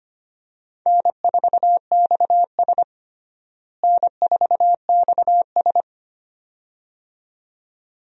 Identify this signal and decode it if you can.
Morse code, heard as a keyed tone, about 25 wpm, 700 Hz. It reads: N4XS N4XS
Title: USA Call Signs - 4 Characters - Repeated 2x - 25wpm